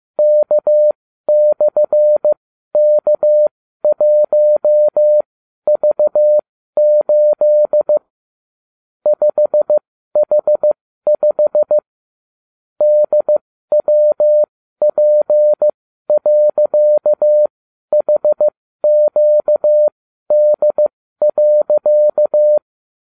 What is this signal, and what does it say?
Morse code, heard as a keyed tone, about 15 wpm, spaced as for 12 wpm, 600 Hz. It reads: K/K1V8 5H5 DWP.HQD.